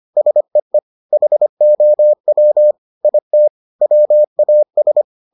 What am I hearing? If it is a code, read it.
Morse code, heard as a keyed tone, about 25 wpm, 600 Hz. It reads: SEE HOW IT WAS